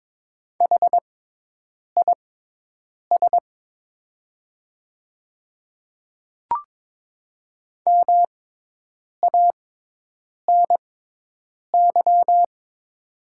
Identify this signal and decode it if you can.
Morse code, heard as a keyed tone, about 22 wpm, spaced as for 8 wpm, 700 Hz. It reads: HIS MANY
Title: 6x Character Spacing - Top 200 Words - 22wpm